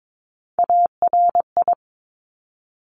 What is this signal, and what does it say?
Morse code, heard as a keyed tone, about 22 wpm, 700 Hz. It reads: ARI
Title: Top Three Letter Combinations in Words 22wpm